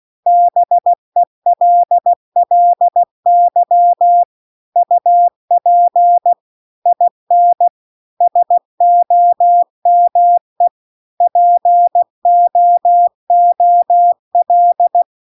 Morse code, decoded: BELLY UP IN SOME POOL

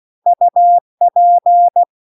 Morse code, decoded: UP

